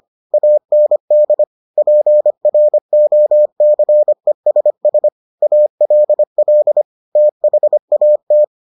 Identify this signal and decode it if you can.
Morse code, heard as a keyed tone, about 25 wpm, 600 Hz. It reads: AND PROCESS ALL THAT